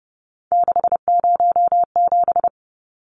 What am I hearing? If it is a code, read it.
Morse code, heard as a keyed tone, about 30 wpm, 700 Hz. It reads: 607
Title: Numbers - 3-Digit 30wpm